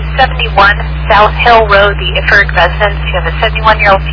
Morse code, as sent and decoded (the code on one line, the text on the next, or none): none